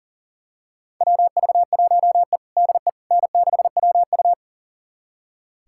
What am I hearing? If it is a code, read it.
Morse code, heard as a keyed tone, about 37 wpm, 700 Hz. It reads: WV1E DE N6WU
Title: USA Callsign DE Callsign - 4 Characters - 40wpm